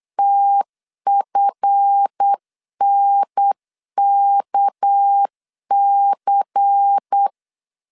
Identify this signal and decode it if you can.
Morse code, heard as a keyed tone, about 8 wpm, 800 Hz. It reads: TFNKC
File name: LEZIONI / 7 GF9Z / 7 GF 8wpm 3108